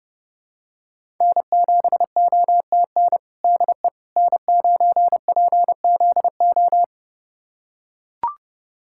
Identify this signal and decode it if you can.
Morse code, heard as a keyed tone, about 30 wpm, 700 Hz. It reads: N7OTN DE N9PZO
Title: USA Callsign DE Callsign - 5 Characters - 30wpm